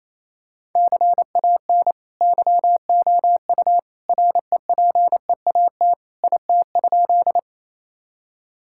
Morse code, heard as a keyed tone, about 28 wpm, 700 Hz. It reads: CAN YOU REPEAT IT?
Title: Sentences from Top 1600 Words 28wpm